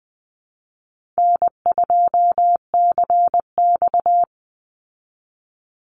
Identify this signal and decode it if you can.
Morse code, heard as a keyed tone, about 20 wpm, 700 Hz. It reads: N2CX
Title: Easy Call Signs Repeated 2x - 20wpm